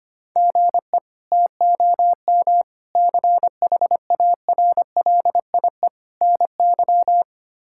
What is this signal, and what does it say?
Morse code, heard as a keyed tone, about 25 wpm, 700 Hz. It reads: GE TOM CHARLIE NY
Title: K1USN SST QSO Practice - 25wpm